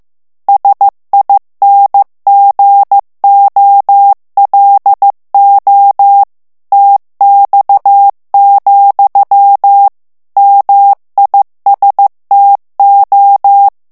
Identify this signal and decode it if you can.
Morse code, heard as a keyed tone, about 15 wpm, 800 Hz. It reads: SINGOLO TX, MISTO
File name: morse mp3 / ITALIEN texte clair 2 vitesse 14 mots minute